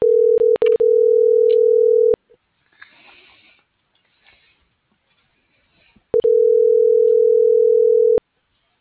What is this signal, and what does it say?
Ringback tone (North America), 440 + 480 Hz.